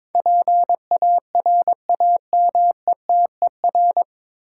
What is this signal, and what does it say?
Morse code, heard as a keyed tone, about 22 wpm, 700 Hz. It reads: PARAMETER
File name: practice / 22wpm / Nine Letter Words - 22wpm